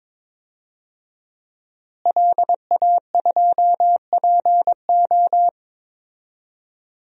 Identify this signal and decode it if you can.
Morse code, heard as a keyed tone, about 22 wpm, 700 Hz. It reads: LA2PO
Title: DXCC-100 Call Signs - 5 Characters - Repeated 3x - 22wpm